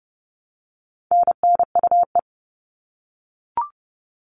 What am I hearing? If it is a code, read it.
Morse code, heard as a keyed tone, about 30 wpm, 700 Hz. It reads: NNUE
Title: All Cut Numbers - 4-Digit 30wpm